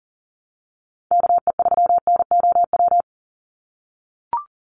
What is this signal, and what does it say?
Morse code, heard as a keyed tone, about 40 wpm, 700 Hz. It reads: KE3NOW
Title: USA Call Signs 40wpm